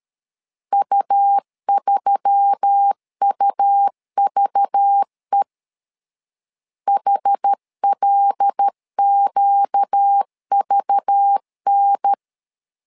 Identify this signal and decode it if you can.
Morse code, heard as a keyed tone, about 13 wpm, 800 Hz. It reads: U3UVE HLQVN
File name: LEZIONI / 5 LBVC / 5 LBV 12wpm 1855